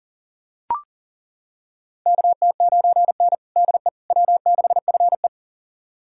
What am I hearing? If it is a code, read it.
Morse code, heard as a keyed tone, about 40 wpm, 700 Hz. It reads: KT9N DE W6FE
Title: USA Callsign DE Callsign - 4 Characters - 40wpm